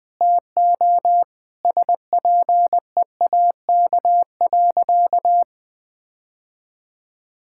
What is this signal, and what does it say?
Morse code, heard as a keyed tone, about 21 wpm, 700 Hz. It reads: TO SPEAK.
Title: Sentences with 5-Word Phrases - Spoken Beginning - 20wpm